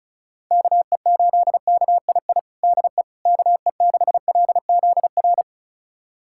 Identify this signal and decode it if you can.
Morse code, heard as a keyed tone, about 36 wpm, 700 Hz. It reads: KE8KII DE KE6LZR
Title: USA Callsign DE Callsign - 6 Characters - 35wpm